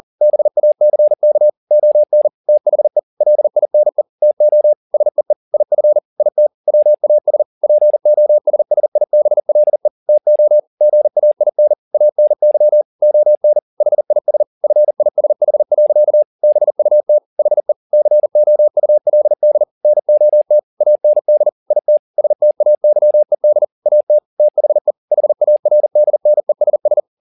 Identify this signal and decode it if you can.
Morse code, heard as a keyed tone, about 40 wpm, 600 Hz. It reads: BACK ON THE LINE TO SEE IF IT WAS POSSIBLE TO GAIN ANY ON HIS FISH. BUT HE COULD NOT AND IT STAYED AT THE HARDNESS
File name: the old man and the sea 2 40wpm